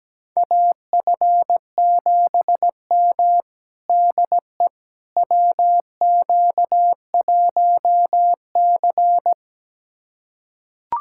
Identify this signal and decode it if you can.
Morse code, heard as a keyed tone, about 17 wpm, 700 Hz. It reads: AF7M DE WQ1C
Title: USA Callsign DE Callsign - 4 Characters - 17wpm